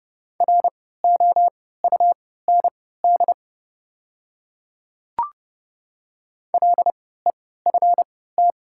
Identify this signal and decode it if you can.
Morse code, heard as a keyed tone, about 30 wpm, spaced as for 17 wpm, 700 Hz. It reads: ROUND LEFT
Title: 3x Character Spacing - Top 300 Words - 30wpm